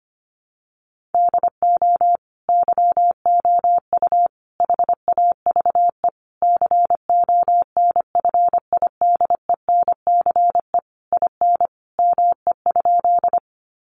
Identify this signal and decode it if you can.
Morse code, heard as a keyed tone, about 25 wpm, 700 Hz. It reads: DO YOU HAVE CONFIDENCE IN ME?
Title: Sentences from Top 1500 Words 25wpm